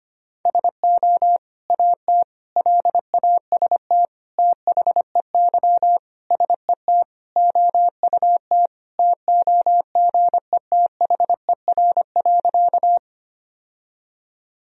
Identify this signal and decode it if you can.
Morse code, heard as a keyed tone, about 25 wpm, 700 Hz. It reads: SO AT LAST THEY SET OUT TOGETHER.
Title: Sentences from Top 1000 Words - Review Part 1 - 25wpm